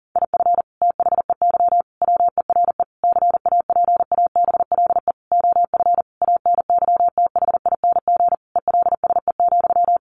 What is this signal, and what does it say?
Morse code, heard as a keyed tone, about 40 wpm, 700 Hz. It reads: IF THEY WERE CAPABLE OF ANYTHING ELSE,
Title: Sentences from Top 1200 Words 40wpm